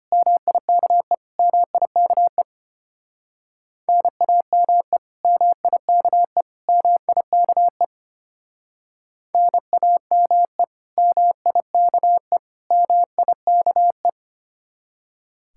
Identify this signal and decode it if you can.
Morse code, heard as a keyed tone, about 33 wpm, 700 Hz. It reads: MIKE MIKE NAME MIKE MIKE NAME MIKE MIKE